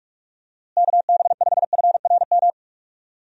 Morse code, heard as a keyed tone, about 44 wpm, 700 Hz. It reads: KB5FRM
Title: USA Call Signs - 6 Characters - Repeated 2x - 45wpm